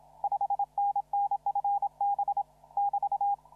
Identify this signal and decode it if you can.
Morse code, heard as a keyed tone, about 26 wpm, 810 Hz. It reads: 5NN FB <BT>
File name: CW / RHOB CW 2017 HA6P 0001